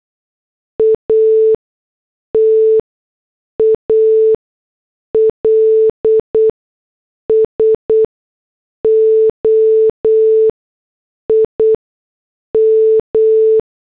Morse code, heard as a keyed tone, about 8 wpm, spaced as for 6 wpm, 440 Hz. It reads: AT ALSO IM